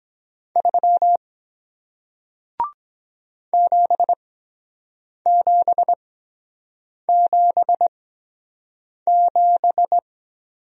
Morse code, heard as a keyed tone, about 23 wpm, spaced as for 7 wpm, 700 Hz. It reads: 3 7777